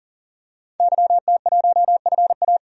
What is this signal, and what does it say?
Morse code, heard as a keyed tone, about 40 wpm, 700 Hz. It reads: YT1FA
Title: DXCC-100 Call Signs - 5 Characters - 40wpm